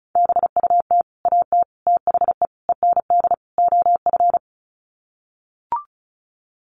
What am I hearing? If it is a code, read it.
Morse code, heard as a keyed tone, about 34 wpm, 700 Hz. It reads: BUT AT THE END OF
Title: Sets of 5 Words Encore2 35wpm